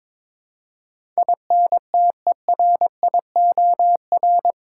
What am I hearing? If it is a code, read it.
Morse code, heard as a keyed tone, about 22 wpm, 700 Hz. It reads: INTERIOR
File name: practice / 22wpm / Eight Letter Words - 22wpm